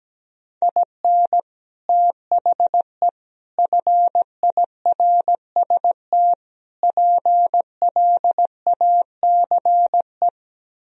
Morse code, heard as a keyed tone, about 17 wpm, 700 Hz. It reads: IN THE FIRST PLACE
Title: Speed-Racing - Sets of 4 Words 17wpm